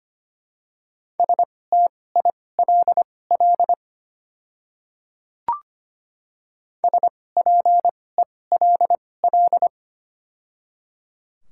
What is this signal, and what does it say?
Morse code, heard as a keyed tone, about 25 wpm, spaced as for 18 wpm, 700 Hz. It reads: STILL SPELL
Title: 2x Character Spacing - Top 300 Words - 25wpm